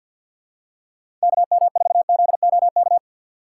Morse code, heard as a keyed tone, about 53 wpm, 700 Hz. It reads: KM4ZOK
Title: USA Call Signs - 6 Characters - 3x Character Spacing - 50wpm